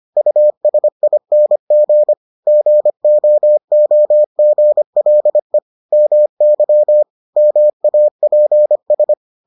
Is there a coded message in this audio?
Yes: Morse code from 0.2 to 9.1 s, heard as a keyed tone, about 25 wpm, 600 Hz. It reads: USING GOOGLE MY MAPS